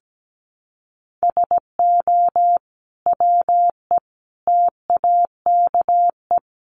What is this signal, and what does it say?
Morse code, heard as a keyed tone, about 17 wpm, 700 Hz. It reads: SO WE TAKE
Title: Sentences from Top 700 Words 17wpm